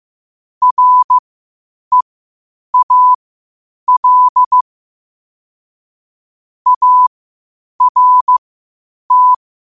Morse code, heard as a keyed tone, about 15 wpm, spaced as for 8 wpm, 1000 Hz. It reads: REAL ART
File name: qotdfiles / 8wpm